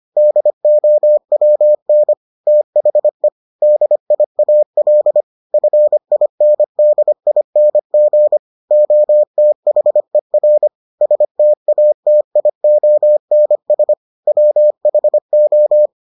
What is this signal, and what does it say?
Morse code, heard as a keyed tone, about 25 wpm, 600 Hz. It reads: DOWN THE DIAL FINDING OTHER STATIONS WHO